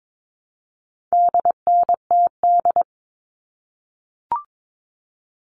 Morse code, heard as a keyed tone, about 22 wpm, 700 Hz. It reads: DNTD
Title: All Cut Numbers - 4-Digit 22wpm